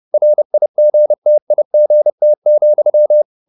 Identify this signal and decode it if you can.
Morse code, heard as a keyed tone, about 30 wpm, 600 Hz. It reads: RIGTIGT,
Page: Ferie 30 wpm